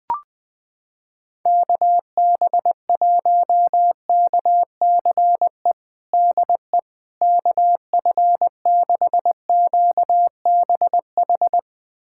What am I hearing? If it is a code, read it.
Morse code, heard as a keyed tone, about 20 wpm, 700 Hz. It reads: KB1KCE DE KF6QBH
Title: USA Callsign DE Callsign 20wpm